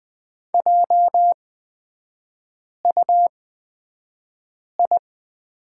Morse code, heard as a keyed tone, about 20 wpm, spaced as for 5 wpm, 700 Hz. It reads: JUI
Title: 243 Lesson 39e - Letter z - Three Characters 20wpm